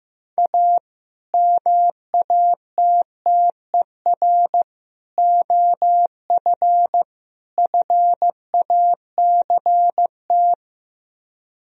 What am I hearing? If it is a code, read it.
Morse code, heard as a keyed tone, about 15 wpm, 700 Hz. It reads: A MATTER OF FACT
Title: Speed-Racing - Sets of 4 Words 15wpm